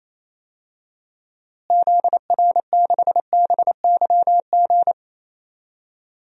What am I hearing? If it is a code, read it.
Morse code, heard as a keyed tone, about 28 wpm, 700 Hz. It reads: ZR6BYG
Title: DXCC-100 Call Signs - 6 Characters - Repeated 2x - 28wpm